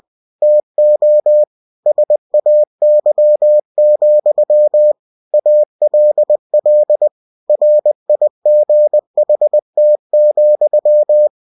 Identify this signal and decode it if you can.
Morse code, heard as a keyed tone, about 20 wpm, 600 Hz. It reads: TO SAY, ALL RIGHT,
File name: cw-books / winnie-the-pooh / winnie-20wpm-page-0037